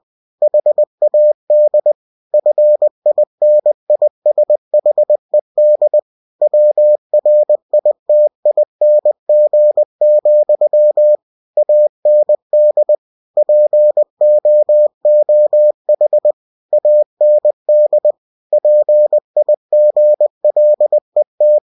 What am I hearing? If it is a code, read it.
Morse code, heard as a keyed tone, about 20 wpm, 600 Hz. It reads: HAD FINISHED WRITING, AND POOH AND PIGLET
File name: cw-books / winnie-the-pooh / winnie-20wpm-page-0063